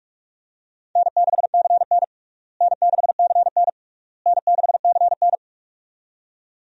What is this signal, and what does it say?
Morse code, heard as a keyed tone, about 46 wpm, 700 Hz. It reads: N6CN N6CN N6CN
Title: USA Call Signs - 4 Characters - Repeated 3x - 45wpm